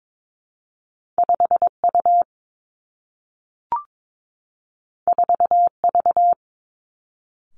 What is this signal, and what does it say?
Morse code, heard as a keyed tone, about 22 wpm, 700 Hz. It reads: 5 U 4 V
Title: Two Challenging Letter-Number Combinations 22wpm